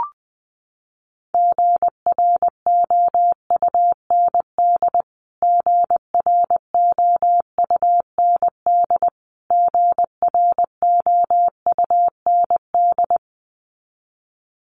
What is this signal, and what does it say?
Morse code, heard as a keyed tone, about 20 wpm, 700 Hz. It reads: GROUND GROUND GROUND
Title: Top 500 Words Repeated 20wpm